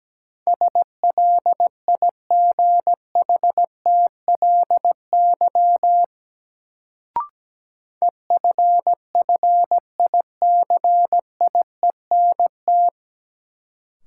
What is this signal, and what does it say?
Morse code, heard as a keyed tone, about 17 wpm, 700 Hz. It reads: SLIGHTLY EFFICIENT
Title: Sentences from Top 1300 Words - New Word Drill - 17wpm